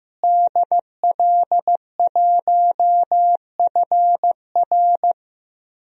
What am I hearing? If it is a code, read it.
Morse code, heard as a keyed tone, about 15 wpm, 700 Hz. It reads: DL1FR
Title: DXCC-100 Call Signs - 5 Characters - Repeated 2x - 15wpm